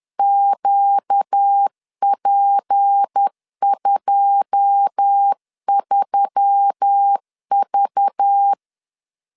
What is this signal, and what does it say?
Morse code, heard as a keyed tone, about 10 wpm, 800 Hz. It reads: QP23V
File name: LEZIONI / 9 2867 10wpm 2733